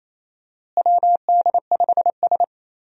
Morse code, heard as a keyed tone, about 28 wpm, 700 Hz. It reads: WD5S
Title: USA Call Signs - 4 Characters - Repeated 2x - 28wpm